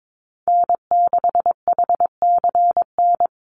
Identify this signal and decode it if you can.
Morse code, heard as a keyed tone, about 22 wpm, 700 Hz. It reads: N6HCN
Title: Call Signs Repeated 2x - 22wpm